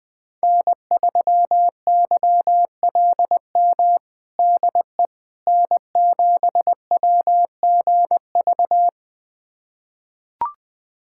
Morse code, heard as a keyed tone, about 20 wpm, 700 Hz. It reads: N3YLM DE N7WGV
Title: USA Callsign DE Callsign - 5 Characters - 20wpm